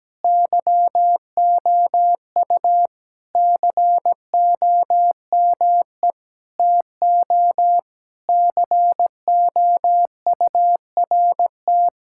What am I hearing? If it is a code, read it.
Morse code, heard as a keyed tone, about 17 wpm, 700 Hz. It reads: YOU COME TO COURT
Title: Sentences from Top 1600 Words 17wpm